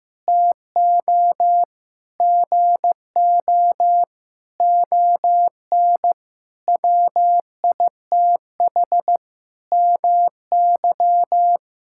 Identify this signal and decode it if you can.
Morse code, heard as a keyed tone, about 15 wpm, 700 Hz. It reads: TO GO ON WITH MY
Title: Sentences from Top 1000 Words - Review Part 6 - 15wpm